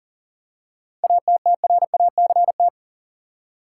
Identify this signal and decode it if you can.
Morse code, heard as a keyed tone, about 40 wpm, 700 Hz. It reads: ATTRACT
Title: Seven Letter Words - 40wpm